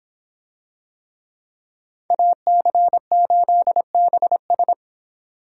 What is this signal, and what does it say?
Morse code, heard as a keyed tone, about 25 wpm, 700 Hz. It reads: AC8BS